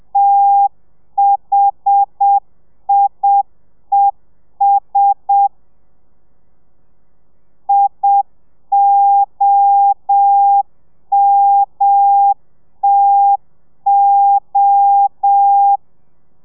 Morse code, heard as a keyed tone, about 7 wpm, 800 Hz. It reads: THIES IOMTO
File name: telegrafia / i2tfj / Lez1 30